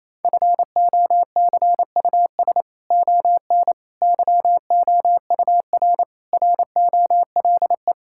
Morse code, heard as a keyed tone, about 28 wpm, 700 Hz. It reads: FOCUS ON YOUR ROLE
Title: Sentences from Top 1000 Words - Review Part 6 - 28wpm